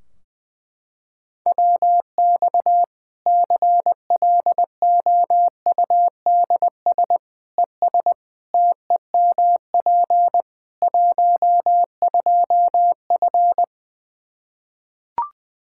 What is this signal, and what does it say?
Morse code, heard as a keyed tone, about 20 wpm, 700 Hz. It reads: WX CLOUDS ES TEMP 12F
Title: K1USN SST QSO Practice 20wpm